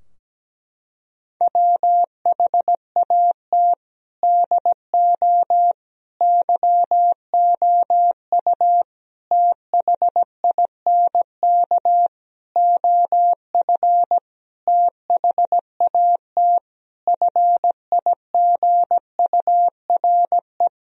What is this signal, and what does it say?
Morse code, heard as a keyed tone, about 17 wpm, 700 Hz. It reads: WHAT DO YOU THINK OF THAT FIGURE
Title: Sentences from Top 1000 Words - Review Part 2 - 17wpm